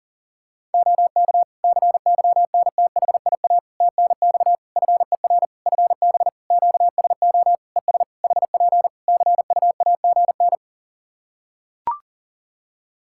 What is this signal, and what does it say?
Morse code, heard as a keyed tone, about 40 wpm, 700 Hz. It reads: OK CYNTHIA TNX FER FB QSO ES HP CUAGN
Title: K1USN SST QSO Practice 40wpm